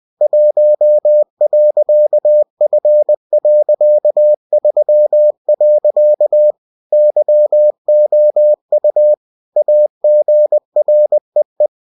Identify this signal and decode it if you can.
Morse code, heard as a keyed tone, about 20 wpm, 600 Hz. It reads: 1.F.3. YOU AGREE